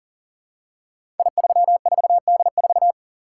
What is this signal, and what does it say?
Morse code, heard as a keyed tone, about 39 wpm, 700 Hz. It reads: I34D4
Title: Code Groups - 5 Letter:Numbers - 40wpm